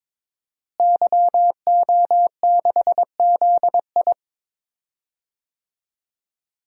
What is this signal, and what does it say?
Morse code, heard as a keyed tone, about 21 wpm, 700 Hz. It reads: YO6ZI
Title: DXCC-100 Call Signs - 5 Characters - Repeated 2x - 22wpm